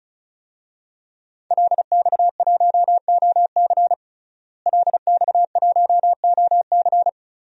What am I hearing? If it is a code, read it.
Morse code, heard as a keyed tone, about 36 wpm, 700 Hz. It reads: LX1OC LX1OC
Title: DXCC-100 Call Signs - 5 Characters - Repeated 2x - 35wpm